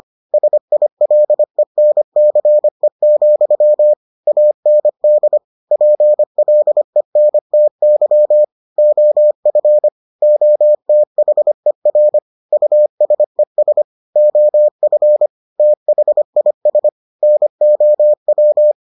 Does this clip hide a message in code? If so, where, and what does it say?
Yes: Morse code from 0.3 to 18.7 s, heard as a keyed tone, about 25 wpm, 600 Hz. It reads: SILENCE, AND PLENTY OF OTHER USES OF THIS NOW